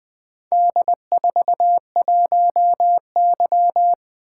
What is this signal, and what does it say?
Morse code, heard as a keyed tone, about 20 wpm, 700 Hz. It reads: D41Y
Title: Four Letter-Number Combinations - 20wpm